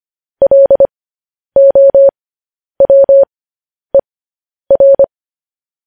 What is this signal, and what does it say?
Morse code, heard as a keyed tone, about 25 wpm, spaced as for 10 wpm, 550 Hz. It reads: LOWER
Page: news-headlines-in-morse-code-at-10-wpm